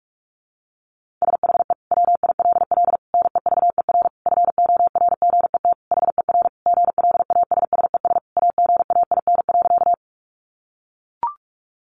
Morse code, heard as a keyed tone, about 44 wpm, 700 Hz. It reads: SHE WILL NEVER FORGET HER GLASSES AGAIN.
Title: Additional Easy Phrases 45wpm